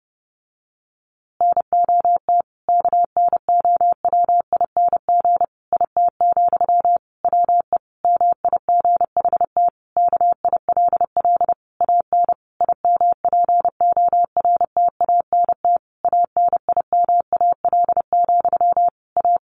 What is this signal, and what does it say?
Morse code, heard as a keyed tone, about 30 wpm, 700 Hz. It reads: NOT KNOWING IT, WE MIGHT KILL AN IMPORTANT ANIMAL, A
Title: A Sound Of Thunder 30wpm